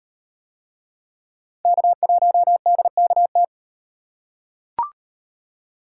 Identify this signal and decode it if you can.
Morse code, heard as a keyed tone, about 38 wpm, 700 Hz. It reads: K1DKT